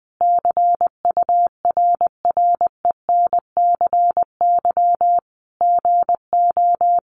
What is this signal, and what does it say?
Morse code, heard as a keyed tone, about 20 wpm, 700 Hz. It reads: CURRENCY GO
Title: Sentences from Top 1500 Words 20wpm